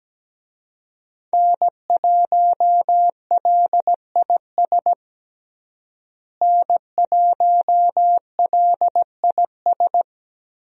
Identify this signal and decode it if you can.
Morse code, heard as a keyed tone, about 17 wpm, 700 Hz. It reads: N1LIS N1LIS
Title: USA Call Signs - 5 Characters - Repeated 3x - 17wpm